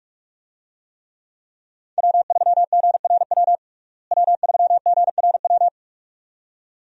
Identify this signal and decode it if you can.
Morse code, heard as a keyed tone, about 41 wpm, 700 Hz. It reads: W3GRW W3GRW
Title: USA Call Signs - 5 Characters - Repeated 2x - 45wpm